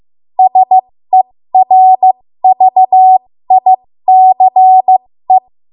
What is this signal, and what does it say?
Morse code, heard as a keyed tone, about 15 wpm, spaced as for 13 wpm, 750 Hz. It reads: SERVICE
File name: Morse / 13WPM